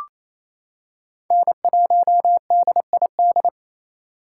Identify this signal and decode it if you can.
Morse code, heard as a keyed tone, about 28 wpm, 700 Hz. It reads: N1DID
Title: K1USN SST QSO Practice - 28wpm